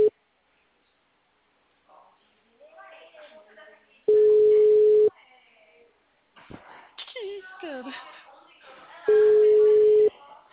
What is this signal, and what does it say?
Ringback tone (Europe, 425 Hz), with speech.
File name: out- -2053- - - .90813